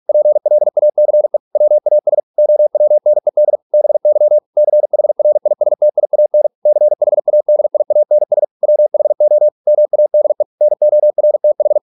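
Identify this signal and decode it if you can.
Morse code, heard as a keyed tone, about 46 wpm, 600 Hz. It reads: PLACE WAS OWNED BY CHRISTIAN CHADIANS WHO MADE NORTH
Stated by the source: Morse practice at 45 wpm, the novel Never by Ken Follett in English, from CW med Gnister.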